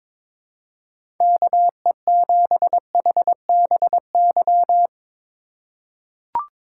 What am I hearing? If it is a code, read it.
Morse code, heard as a keyed tone, about 22 wpm, 700 Hz. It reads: KE7HBY
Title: USA Call Signs - 6 Characters - 6x Character Spacing - 22wpm